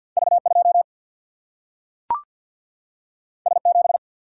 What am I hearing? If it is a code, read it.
Morse code, heard as a keyed tone, about 53 wpm, 700 Hz. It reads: V 2 S 7
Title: Single-Letter Single-Number 50wpm